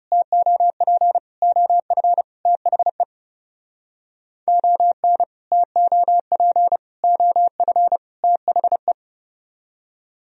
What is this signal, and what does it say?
Morse code, heard as a keyed tone, about 37 wpm, 700 Hz. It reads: TOP OF THE ON TOP OF THE
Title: Speed-Racing - Sets of 4 Words 30wpm